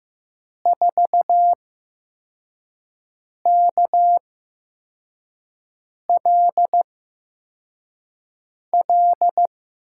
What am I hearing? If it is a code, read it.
Morse code, heard as a keyed tone, about 15 wpm, spaced as for 4 wpm, 700 Hz. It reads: 4KLL
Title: USA Call Signs - 6 Characters - 8x Character Spacing - 15wpm